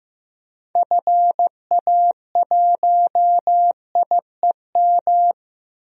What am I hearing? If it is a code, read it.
Morse code, heard as a keyed tone, about 15 wpm, 700 Hz. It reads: FA1IEM
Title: DXCC-100 Call Signs - 6 Characters - 15wpm